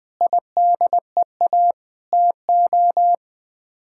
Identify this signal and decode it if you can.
Morse code, heard as a keyed tone, about 20 wpm, 700 Hz. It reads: IDEA TO
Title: Sets of 8 Words - 20wpm